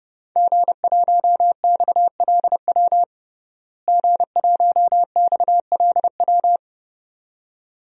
Morse code, heard as a keyed tone, about 32 wpm, 700 Hz. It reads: G1XLW G1XLW
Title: DXCC-100 Call Signs - 5 Characters - Repeated 2x - 30wpm